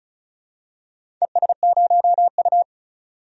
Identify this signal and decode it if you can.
Morse code, heard as a keyed tone, about 36 wpm, 700 Hz. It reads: ES0U
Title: DXCC-100 Call Signs - 4 Characters - 35wpm